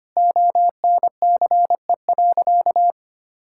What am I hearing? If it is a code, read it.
Morse code, heard as a keyed tone, about 25 wpm, 700 Hz. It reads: ONCE.
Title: Sentences from Top 1600 Words 25wpm